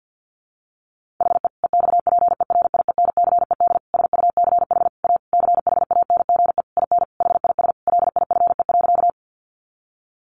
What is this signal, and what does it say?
Morse code, heard as a keyed tone, about 50 wpm, 700 Hz. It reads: HE EXPERIENCED SUCH A CHANGE IN HIS LIFE.
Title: Sentences from Top 1100 Words 50wpm